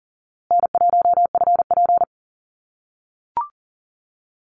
Morse code, heard as a keyed tone, about 40 wpm, 700 Hz. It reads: N1FP
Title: USA Call Signs Repeated 3x 40wpm